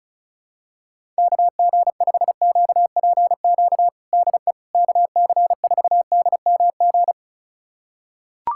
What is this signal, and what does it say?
Morse code, heard as a keyed tone, about 36 wpm, 700 Hz. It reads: KG5QPQ DE KC4DMG
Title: USA Callsign DE Callsign - 6 Characters - 35wpm